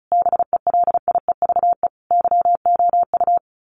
Morse code, heard as a keyed tone, about 34 wpm, 700 Hz. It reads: BELIEVE YOU
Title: Sets of 5 Words Encore 35wpm